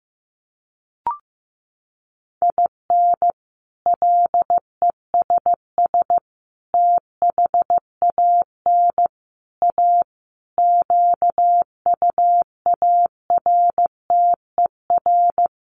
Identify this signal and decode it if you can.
Morse code, heard as a keyed tone, about 15 wpm, 700 Hz. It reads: IN LESS THAN A QUARTER
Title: Sentences from Top 1300 Words 15wpm